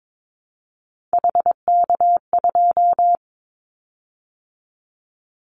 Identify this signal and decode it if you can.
Morse code, heard as a keyed tone, about 22 wpm, 700 Hz. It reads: HK2
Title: Callsign Prefixes for Top 100 DXCC - 22wpm